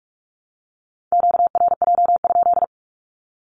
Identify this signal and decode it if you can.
Morse code, heard as a keyed tone, about 45 wpm, 700 Hz. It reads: QRJ?
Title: Q-Codes Repeated 3x 45wpm